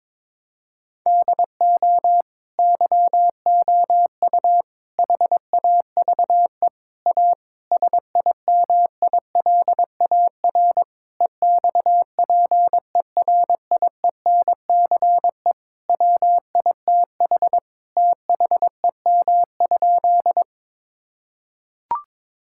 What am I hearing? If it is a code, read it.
Morse code, heard as a keyed tone, about 22 wpm, 700 Hz. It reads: DO YOU HAVE A SIMILAR EXPERIENCE WITH THEM?
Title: Sentences from Top 1000 Words - Review Part 5 - 22wpm